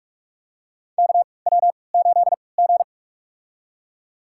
Morse code, heard as a keyed tone, about 41 wpm, spaced as for 25 wpm, 700 Hz. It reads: KW8G
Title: USA Call Signs - 4 Characters - 3x Character Spacing - 45wpm